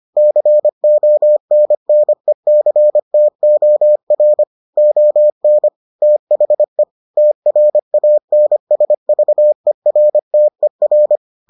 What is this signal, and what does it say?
Morse code, heard as a keyed tone, about 25 wpm, 600 Hz. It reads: CONNECTOR ON THE TRANSVERTER